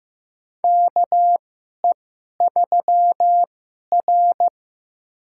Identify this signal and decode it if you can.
Morse code, heard as a keyed tone, about 15 wpm, spaced as for 11 wpm, 700 Hz. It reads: KE3R
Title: USA Call Signs - 4 Characters - 2x Character Spacing - 15wpm